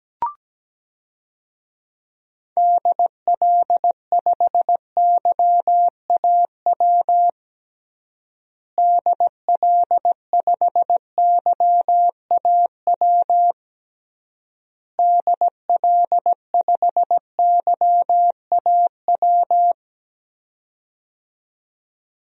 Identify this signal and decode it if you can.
Morse code, heard as a keyed tone, about 17 wpm, 700 Hz. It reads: DL5YAW DL5YAW DL5YAW
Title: DXCC-100 Call Signs - 6 Characters - Repeated 3x - 17wpm